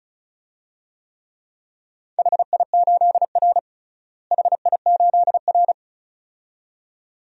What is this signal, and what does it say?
Morse code, heard as a keyed tone, about 36 wpm, 700 Hz. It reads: HI8R HI8R
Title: DXCC-100 Call Signs - 4 Characters - Repeated 2x - 35wpm